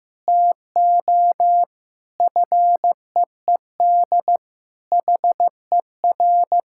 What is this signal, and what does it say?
Morse code, heard as a keyed tone, about 15 wpm, 700 Hz. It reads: TO FEED HER
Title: Sentences from Top 1600 Words 15wpm